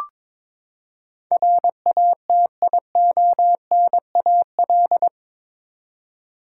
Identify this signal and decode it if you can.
Morse code, heard as a keyed tone, about 22 wpm, 700 Hz. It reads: RATIONAL
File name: practice / 22wpm / Eight Letter Words - 22wpm